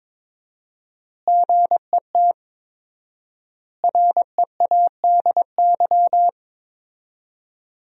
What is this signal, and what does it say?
Morse code, heard as a keyed tone, about 22 wpm, 700 Hz. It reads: GET READY
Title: 4x Word-Spacing - Sentences from Top 400 Words - 22wpm